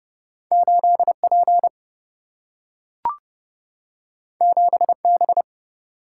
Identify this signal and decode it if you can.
Morse code, heard as a keyed tone, about 30 wpm, 700 Hz. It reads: 8P 7B
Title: Single-Letter Single-Number 30wpm